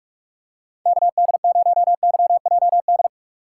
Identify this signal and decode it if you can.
Morse code, heard as a keyed tone, about 46 wpm, 700 Hz. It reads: KD0YJD
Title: USA Call Signs - 6 Characters - 4x Character Spacing - 45wpm